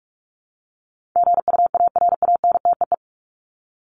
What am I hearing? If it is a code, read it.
Morse code, heard as a keyed tone, about 44 wpm, 700 Hz. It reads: GUARANTEE
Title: Top 1501-1750 Words 45wpm